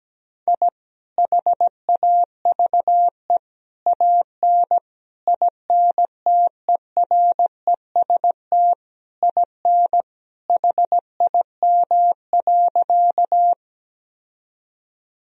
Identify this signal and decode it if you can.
Morse code, heard as a keyed tone, about 17 wpm, 700 Hz. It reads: I HAVE AN INTEREST IN HIM.
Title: Sentences from Top 1000 Words - Review Part 2 - 17wpm